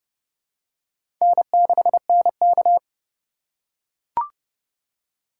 Morse code, heard as a keyed tone, about 30 wpm, 700 Hz. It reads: N6NK
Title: USA Call Signs - 4 Characters - 6x Character Spacing - 30wpm